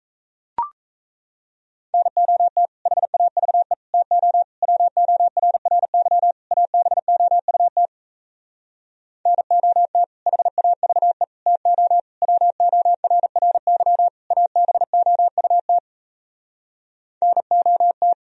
Morse code, heard as a keyed tone, about 37 wpm, 700 Hz. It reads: NOT HAVE TO WORRY ABOUT NOT HAVE TO WORRY ABOUT NOT